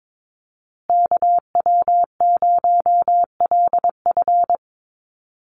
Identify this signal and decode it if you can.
Morse code, heard as a keyed tone, about 22 wpm, 700 Hz. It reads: KW0LF
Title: USA Call Signs Repeated 3x 22wpm